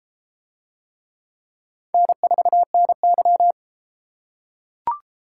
Morse code, heard as a keyed tone, about 33 wpm, 700 Hz. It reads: N4NY